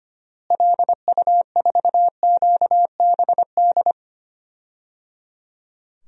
Morse code, heard as a keyed tone, about 25 wpm, 700 Hz. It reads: LU4QBD
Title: DXCC-100 Call Signs - 6 Characters - Repeated 3x - 25wpm